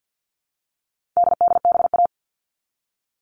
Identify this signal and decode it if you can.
Morse code, heard as a keyed tone, about 50 wpm, 700 Hz. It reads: DDBA
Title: All Cut Numbers - 4-Digit 50wpm